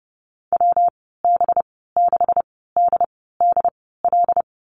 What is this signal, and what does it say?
Morse code, heard as a keyed tone, about 30 wpm, spaced as for 17 wpm, 700 Hz. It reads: WB6DDL
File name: Lesson 37g - Letter x - Call Signs 30wpm